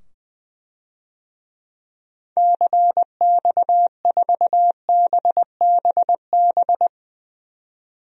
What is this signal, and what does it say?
Morse code, heard as a keyed tone, about 20 wpm, 700 Hz. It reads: CX4BBB
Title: DXCC-100 Call Signs - 6 Characters - 20wpm